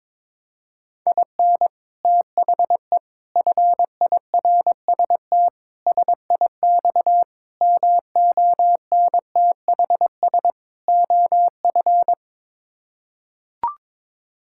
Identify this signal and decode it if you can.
Morse code, heard as a keyed tone, about 22 wpm, 700 Hz. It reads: IN THE FIRST SIX MONTHS OF
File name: practice / 22wpm / Sets of 6 Words - 22wpm